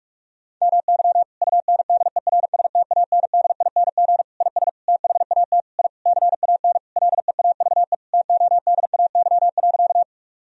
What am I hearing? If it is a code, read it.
Morse code, heard as a keyed tone, about 47 wpm, 700 Hz. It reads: MY UNDERSTANDING IS THAT I CAN LEAVE TODAY.
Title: Sentences from Top 1000 Words - Review Part 3 - 45wpm